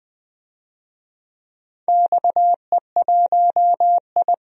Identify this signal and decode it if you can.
Morse code, heard as a keyed tone, about 20 wpm, 700 Hz. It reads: XE1I
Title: DXCC-100 Call Signs - 4 Characters - Repeated 3x - 20wpm